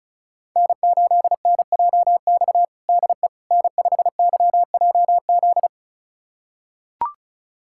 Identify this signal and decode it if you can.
Morse code, heard as a keyed tone, about 34 wpm, 700 Hz. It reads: N8NJX DE N5YJZ
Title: USA Callsign DE Callsign - 5 Characters - 35wpm